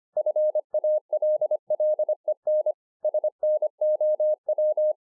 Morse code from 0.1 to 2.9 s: FALLEN